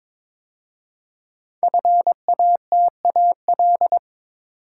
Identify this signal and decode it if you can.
Morse code, heard as a keyed tone, about 22 wpm, 700 Hz. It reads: FATAL
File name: 5 Letter Words - First 3 Letters Given - 22wpm